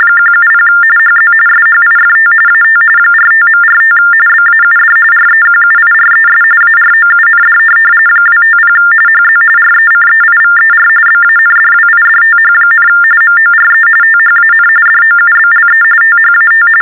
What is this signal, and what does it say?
TTY/TDD text telephone tones, about 45.45 baud. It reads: 170 HZ SHIFT AT 45.45 BAUD REQUIRES AROUND 250 HZ RECEIVER BANDWIDTH, MORE THAN DOUBLE THAT
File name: teletype-RTTY-45